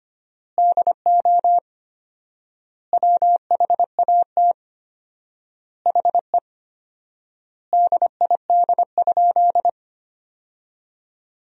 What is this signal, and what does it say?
Morse code, heard as a keyed tone, about 25 wpm, 700 Hz. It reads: DO WHAT HE DID?
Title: 4x Word-Spacing - Sentences from Top 100 Words - 25wpm